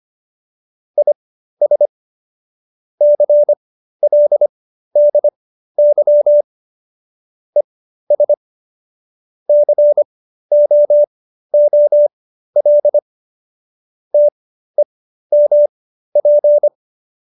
Morse code, heard as a keyed tone, about 25 wpm, spaced as for 13 wpm, 590 Hz. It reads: IS CLDY ES COOL TEMP